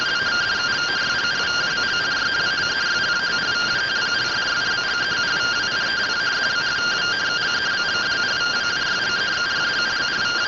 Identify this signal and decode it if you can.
Two-tone teleprinter signal, about 170 Hz shift.